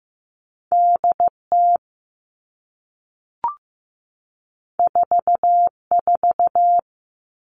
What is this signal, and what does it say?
Morse code, heard as a keyed tone, about 15 wpm, 700 Hz. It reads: DT 44
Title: All Cut Numbers - 2-Digit 15wpm